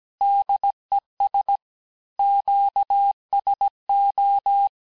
Morse code, heard as a keyed tone, about 17 wpm, 800 Hz. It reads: DES QSO
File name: corrige-cours-1581-fin-du-QSO-vitesse-16-mpm